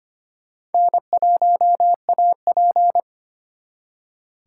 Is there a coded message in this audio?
Yes: Morse code from 0.7 to 3.0 s, heard as a keyed tone, about 32 wpm, 700 Hz. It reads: N1AP